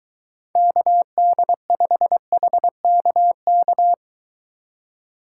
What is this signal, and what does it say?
Morse code, heard as a keyed tone, about 23 wpm, 700 Hz. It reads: KD5HKK